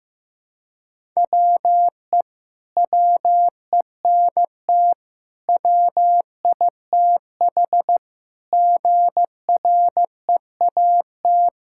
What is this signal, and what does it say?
Morse code, heard as a keyed tone, about 16 wpm, 700 Hz. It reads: WE WENT WITH GREAT
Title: Sentences from Top 1000 Words - Review Part 5 - 15wpm